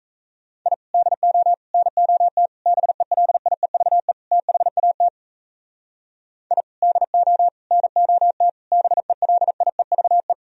Morse code, heard as a keyed tone, about 34 wpm, 700 Hz. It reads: I DO NOT BELIEVE THAT I DO NOT BELIEVE
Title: Speed-Racing - Sets of 5 Words 28wpm